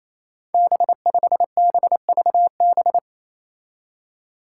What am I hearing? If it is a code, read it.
Morse code, heard as a keyed tone, about 28 wpm, 700 Hz. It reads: B5BVB
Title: Five Challenging Letter-Number Combinations 28wpm